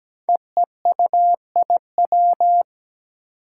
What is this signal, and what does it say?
Morse code, heard as a keyed tone, about 17 wpm, 700 Hz. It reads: EEUIW
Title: Five Challenging Letter-Number Combinations 17wpm